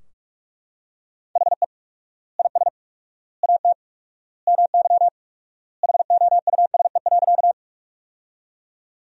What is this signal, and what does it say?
Morse code, heard as a keyed tone, about 42 wpm, 700 Hz. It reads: HE IS AT MY HOUSE.
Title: 4x Word-Spacing - Sentences from Top 200 Words - 45wpm